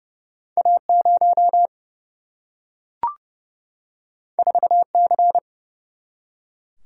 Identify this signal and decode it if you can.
Morse code, heard as a keyed tone, about 30 wpm, 700 Hz. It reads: A 0 4 C
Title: Single-Letter Single-Number 30wpm